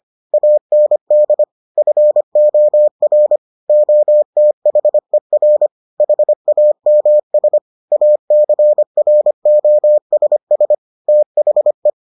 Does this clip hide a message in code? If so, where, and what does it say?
Yes: Morse code from 0.3 to 11.9 s, heard as a keyed tone, about 25 wpm, 600 Hz. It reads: AND FOR OTHER HAMS ACROSS THE